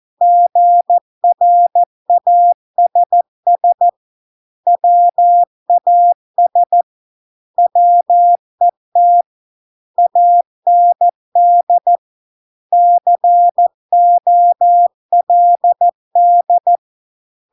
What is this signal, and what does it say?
Morse code, heard as a keyed tone, about 14 wpm, 700 Hz. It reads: GRASS WAS WET AND COLD